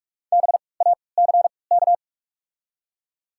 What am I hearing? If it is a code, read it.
Morse code, heard as a keyed tone, about 48 wpm, spaced as for 26 wpm, 700 Hz. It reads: BACK
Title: 3x Character Spacing - Top 200 Words - 45wpm